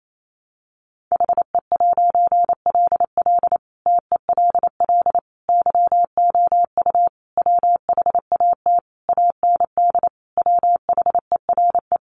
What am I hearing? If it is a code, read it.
Morse code, heard as a keyed tone, about 28 wpm, 700 Hz. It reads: HE'LL TELL YOU WHAT AND WHERE
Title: A Sound Of Thunder 28wpm